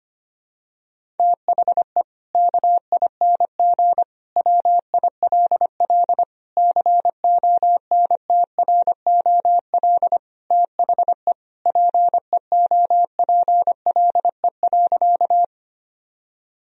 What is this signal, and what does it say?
Morse code, heard as a keyed tone, about 25 wpm, 700 Hz. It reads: THE KING WILL CONTROL THE PEOPLE.
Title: Sentences from Top 1000 Words - Review Part 3 - 25wpm